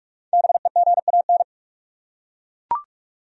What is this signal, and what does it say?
Morse code, heard as a keyed tone, about 41 wpm, 700 Hz. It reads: BEGAN